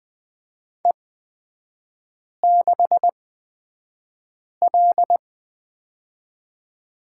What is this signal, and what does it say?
Morse code, heard as a keyed tone, about 20 wpm, spaced as for 5 wpm, 700 Hz. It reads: E6L
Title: 243 Lesson 39e - Letter z - Three Characters 20wpm